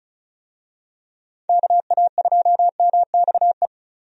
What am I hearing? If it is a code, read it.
Morse code, heard as a keyed tone, about 34 wpm, 700 Hz. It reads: KA2MXE
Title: USA Call Signs - 6 Characters - 35wpm